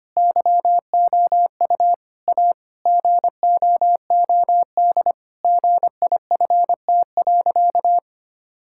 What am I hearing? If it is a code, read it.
Morse code, heard as a keyed tone, about 25 wpm, 700 Hz. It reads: YOU A GOOD GIFT.
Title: Sentences from Top 1000 Words - Review Part 6 - 25wpm